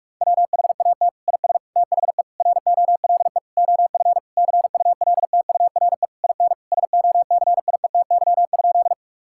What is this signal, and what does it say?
Morse code, heard as a keyed tone, about 41 wpm, 700 Hz. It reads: WHAT IS THE ROLE OF CULTURE IN SOCIETY?
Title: Sentences from Top 1000 Words - Review Part 6 - 45wpm